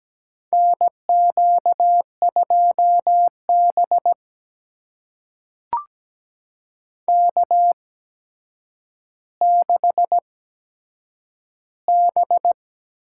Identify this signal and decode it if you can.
Morse code, heard as a keyed tone, about 17 wpm, spaced as for 8 wpm, 700 Hz. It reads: NQ2B K6B
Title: USA Call Signs - 4 Characters - 8x Character Spacing - 17wpm